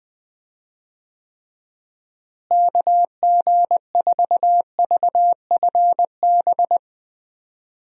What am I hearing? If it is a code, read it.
Morse code, heard as a keyed tone, about 20 wpm, 700 Hz. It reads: KG4VFB
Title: USA Call Signs - 6 Characters - Repeated 2x - 20wpm